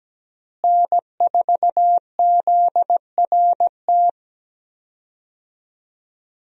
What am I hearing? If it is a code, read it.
Morse code, heard as a keyed tone, about 17 wpm, 700 Hz. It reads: N4ZRT
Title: USA Call Signs - 5 Characters - Repeated 3x - 17wpm